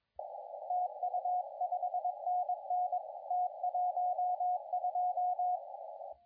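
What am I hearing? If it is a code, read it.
Morse code, heard as a keyed tone, about 23 wpm, 720 Hz, with radio noise. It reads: TU 5NN T12
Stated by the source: CW contest audio recorded at LA4C during the SAC CW contest.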